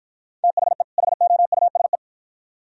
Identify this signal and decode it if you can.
Morse code, heard as a keyed tone, about 53 wpm, 700 Hz. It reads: THE HOUSE